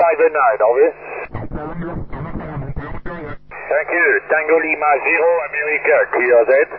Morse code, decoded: none